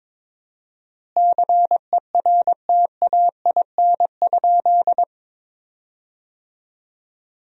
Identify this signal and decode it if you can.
Morse code, heard as a keyed tone, about 22 wpm, 700 Hz. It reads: CERTAIN?
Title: 4x Word-Spacing - Sentences from Top 500 Words - 22wpm